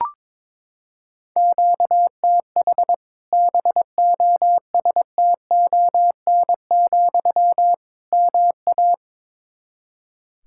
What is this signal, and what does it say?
Morse code, heard as a keyed tone, about 22 wpm, 700 Hz. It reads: QTH BOSTON, MA
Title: Practice QSO - Exchange 1 - Beginning - 22wpm